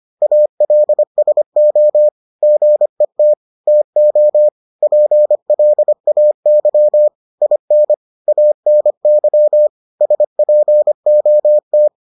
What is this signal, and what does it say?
Morse code, heard as a keyed tone, about 25 wpm, 600 Hz. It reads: ALSO GET TO PLAY IN ANY SPOT